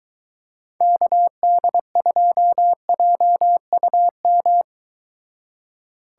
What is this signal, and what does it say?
Morse code, heard as a keyed tone, about 22 wpm, 700 Hz. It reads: KD2JUM